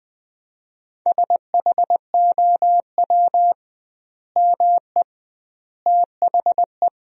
Morse code, heard as a keyed tone, about 20 wpm, 700 Hz. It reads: SHOW ME THE
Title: 2x Word-Spacing - Sentences from Top 200 Words - 20wpm